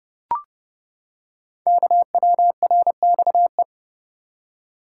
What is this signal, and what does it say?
Morse code, heard as a keyed tone, about 28 wpm, 700 Hz. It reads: KWRXE
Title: Code Groups - 5 Letters - 30wpm